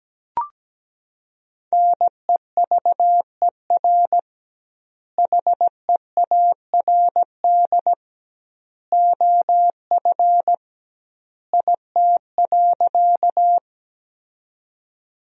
Morse code, heard as a keyed tone, about 17 wpm, 700 Hz. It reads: NEVER HEARD OF IT.
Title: 2x Word-Spacing - Sentences from Top 400 Words - 17wpm